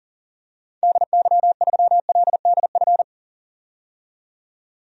Morse code, heard as a keyed tone, about 37 wpm, 700 Hz. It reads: DY3LDF
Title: DXCC-100 Call Signs - 6 Characters - 40wpm